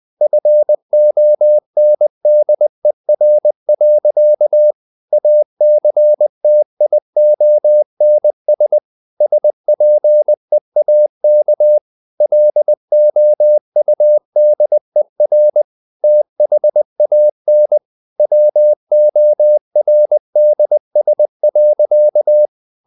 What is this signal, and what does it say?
Morse code, heard as a keyed tone, about 20 wpm, 600 Hz. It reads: FONDER. ACTIONS SPEAK LOUDER THAN WORDS.
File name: CW / Practice / EnglishSayings20